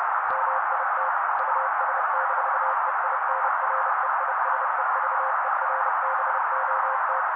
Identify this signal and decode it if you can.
Morse code, heard as a keyed tone, about 30 wpm, 530 Hz. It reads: MIT US-EINREISEVERBOT
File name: daily / 29wpm noise